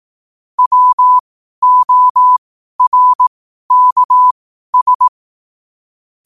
Morse code, heard as a keyed tone, about 18 wpm, spaced as for 12 wpm, 1000 Hz. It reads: WORKS
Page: 12 wpm morse code quotes